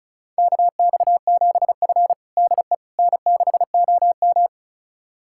Morse code, heard as a keyed tone, about 36 wpm, 700 Hz. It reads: KX7F DE N6OM
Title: USA Callsign DE Callsign - 4 Characters - 35wpm